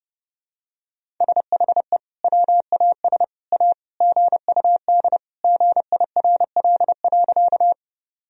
Morse code, heard as a keyed tone, about 32 wpm, 700 Hz. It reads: SHE WAS A GUD GIRL.
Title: Sentences from Top 400 Words and Abbreviations 30wpm